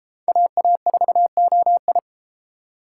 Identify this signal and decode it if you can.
Morse code, heard as a keyed tone, about 31 wpm, 700 Hz. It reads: AA4OI